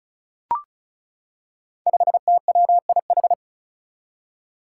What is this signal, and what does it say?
Morse code, heard as a keyed tone, about 36 wpm, 700 Hz. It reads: 5TWIH
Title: Five Challenging Letter-Number Combinations 35wpm